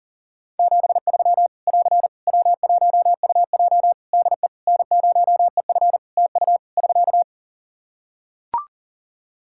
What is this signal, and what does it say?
Morse code, heard as a keyed tone, about 37 wpm, 700 Hz. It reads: 73 <AR> W1UJ DE N0EF TU <SK>
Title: Practice QSO - Exchange 4 - Ending - 40wpm